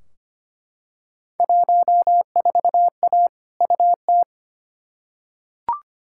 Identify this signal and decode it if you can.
Morse code, heard as a keyed tone, about 25 wpm, 700 Hz. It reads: 14A UT
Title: Field Day Exchanges Encore - 25wpm